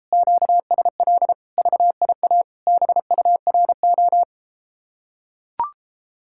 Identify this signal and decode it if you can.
Morse code, heard as a keyed tone, about 33 wpm, 700 Hz. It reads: QSL VIA BURO